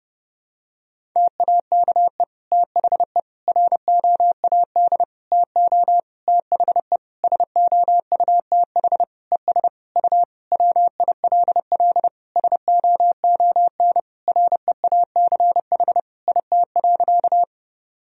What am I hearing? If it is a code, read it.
Morse code, heard as a keyed tone, about 28 wpm, 700 Hz. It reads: TAKE THE ROAD TO THE SOUTH ES U WILL SOON REACH IT.
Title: Sentences from Top 400 Words and Abbreviations 30wpm